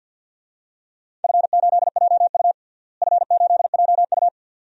Morse code, heard as a keyed tone, about 50 wpm, 700 Hz. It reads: F8JU F8JU
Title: DXCC-100 Call Signs - 4 Characters - Repeated 2x - 50wpm